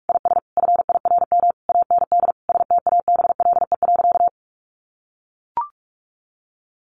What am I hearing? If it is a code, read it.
Morse code, heard as a keyed tone, about 44 wpm, 700 Hz. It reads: IS FIRM AND STABLE.
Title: Sentences from Top 1500 Words 45wpm